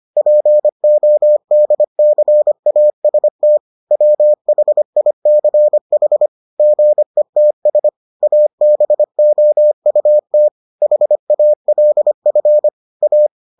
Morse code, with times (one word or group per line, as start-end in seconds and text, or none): 0.2-3.6 PODCAST
3.9-6.3 WHICH
6.6-7.9 GETS
8.2-10.5 ABOUT
10.8-12.7 HALF
13.0-13.3 A